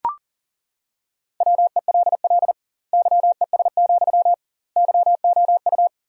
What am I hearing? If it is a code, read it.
Morse code, heard as a keyed tone, about 40 wpm, 700 Hz. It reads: WELL YES, YOU
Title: 2x Word-Spacing - Sentences from Top 500 Words - 40wpm